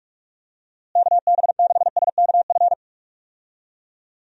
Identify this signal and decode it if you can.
Morse code, heard as a keyed tone, about 44 wpm, 700 Hz. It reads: KB6SKF
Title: USA Call Signs - 6 Characters - Repeated 2x - 45wpm